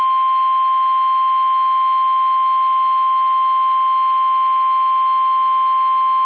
Emergency Alert System (SAME) tones (the 1050 Hz attention tone). It is heard from 0.7 to 5.6 s.